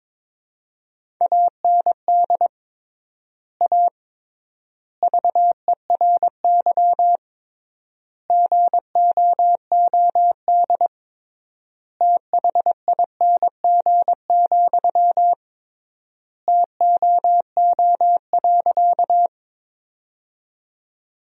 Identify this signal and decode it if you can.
Morse code, heard as a keyed tone, about 22 wpm, 700 Hz. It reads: AND A VERY GOOD THING, TOO.
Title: 3x Word-Spacing - Sentences from Top 200 Words - 22wpm